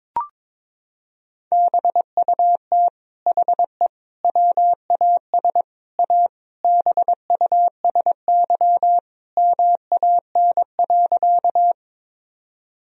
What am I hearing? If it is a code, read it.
Morse code, heard as a keyed tone, about 22 wpm, 700 Hz. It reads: BUT HE WAS A BUSY MAN.
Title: Sentences from Top 1000 Words - Review Part 3 - 22wpm